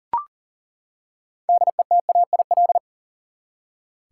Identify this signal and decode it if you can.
Morse code, heard as a keyed tone, about 40 wpm, 700 Hz. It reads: DETAIL
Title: Six Letter Words - 40wpm